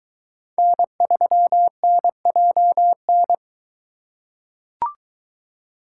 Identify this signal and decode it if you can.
Morse code, heard as a keyed tone, about 22 wpm, 700 Hz. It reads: N3NJN